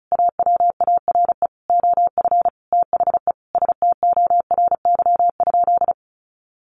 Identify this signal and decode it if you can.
Morse code, heard as a keyed tone, about 34 wpm, 700 Hz. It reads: AWARE OF THE STORY?
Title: Sentences from Top 1200 Words 35wpm